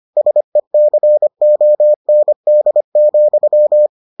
Morse code, heard as a keyed tone, about 25 wpm, 600 Hz. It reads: SECOND,